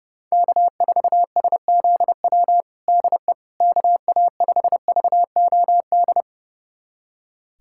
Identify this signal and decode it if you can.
Morse code, heard as a keyed tone, about 32 wpm, 700 Hz. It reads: K4SZW DE KA5VOD
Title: USA Callsign DE Callsign 30wpm